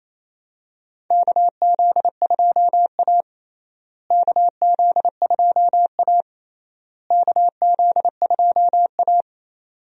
Morse code, heard as a keyed tone, about 28 wpm, 700 Hz. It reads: KZ2A KZ2A KZ2A
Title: USA Call Signs - 4 Characters - Repeated 3x - 28wpm